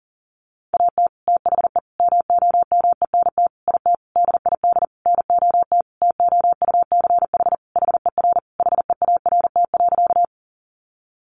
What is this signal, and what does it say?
Morse code, heard as a keyed tone, about 37 wpm, 700 Hz. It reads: AT THE MOMENT IT DID NOT TOUCH HER HEART.
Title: Sentences from Top 1200 Words 40wpm